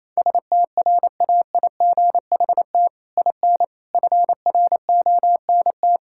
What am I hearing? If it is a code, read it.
Morse code, heard as a keyed tone, about 27 wpm, 700 Hz. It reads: STRAIGHT IN FRONT
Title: Sentences from Top 1000 Words - Review Part 6 - 28wpm